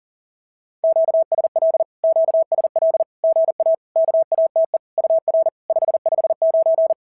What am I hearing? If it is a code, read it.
Morse code, heard as a keyed tone, about 40 wpm, 690 Hz. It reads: QSL QSL GA KATE UR 559
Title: SOTA QSO Practice 40wpm